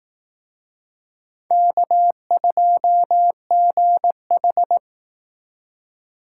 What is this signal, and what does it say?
Morse code, heard as a keyed tone, about 18 wpm, 700 Hz. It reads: K2GH
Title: Speed Racing - USA 1x2 Call Signs 15wpm